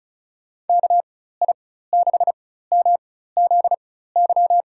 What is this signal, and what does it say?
Morse code, heard as a keyed tone, about 36 wpm, spaced as for 16 wpm, 700 Hz. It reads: KI6MZY
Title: USA Call Signs - 6 Characters - 4x Character Spacing - 35wpm